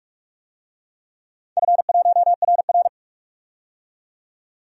Morse code, heard as a keyed tone, about 44 wpm, 700 Hz. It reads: F1RR
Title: DXCC-100 Call Signs - 4 Characters - 45wpm